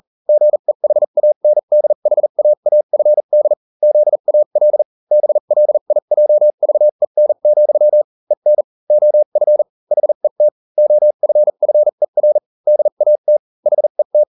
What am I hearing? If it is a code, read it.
Morse code, heard as a keyed tone, about 40 wpm, spaced as for 35 wpm, 600 Hz. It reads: GEHANDHAAFD ZAL BLIJVEN, EN OF HET OFFER DAT HET